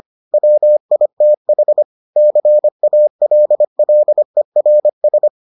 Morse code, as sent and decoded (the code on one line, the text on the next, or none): .-- .. - .... / -.-. .- .-.. .-.. .-. ...
WITH CALLERS